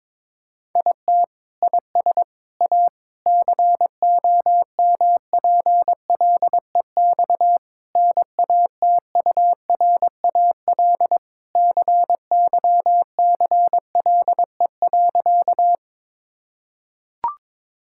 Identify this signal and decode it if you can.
Morse code, heard as a keyed tone, about 21 wpm, 700 Hz. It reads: IT IS A COMPLEX NATURAL CYCLE.
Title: Sentences from Top 1000 Words - Review Part 6 - 22wpm